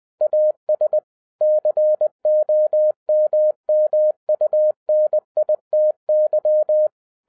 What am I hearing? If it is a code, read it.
Morse code, heard as a keyed tone, about 20 wpm, 610 Hz. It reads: AS COMMUNITY
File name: 2011-07-mp3 / n33 THE 3.11 2 20wpm